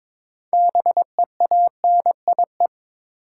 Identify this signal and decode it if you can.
Morse code, heard as a keyed tone, about 22 wpm, 700 Hz. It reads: BEANIE
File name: Difficult Words with Hints - 22wpm